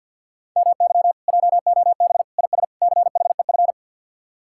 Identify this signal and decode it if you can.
Morse code, heard as a keyed tone, about 63 wpm, 700 Hz. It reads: MY JOB IS CHEF